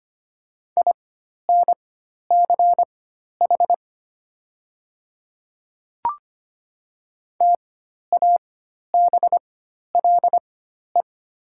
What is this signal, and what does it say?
Morse code, heard as a keyed tone, about 25 wpm, spaced as for 12 wpm, 700 Hz. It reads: INCH TABLE
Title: 4x Character Spacing - Top 500 Words - 25wpm